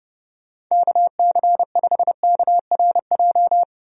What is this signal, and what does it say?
Morse code, heard as a keyed tone, about 30 wpm, 700 Hz. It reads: KC5KRJ
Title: USA Call Signs - 6 Characters - 30wpm